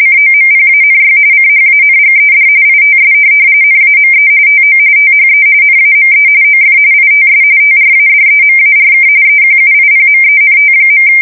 RTTY, about 45.45 baud. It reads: COMPANY CHANGED ITS NAME TO THE NAME OF ITS TRADEMARK EQUIPMENT.
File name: fsk170 / teletype-corp